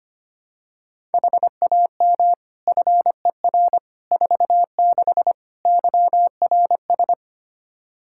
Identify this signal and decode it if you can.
Morse code, heard as a keyed tone, about 25 wpm, 700 Hz. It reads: HAM FER 46 YRS
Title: Practice QSO - Exchange 3 - More Info - 25wpm